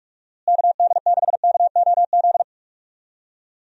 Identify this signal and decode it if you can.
Morse code, heard as a keyed tone, about 46 wpm, 700 Hz. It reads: KD6KOZ
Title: USA Call Signs - 6 Characters - Repeated 2x - 45wpm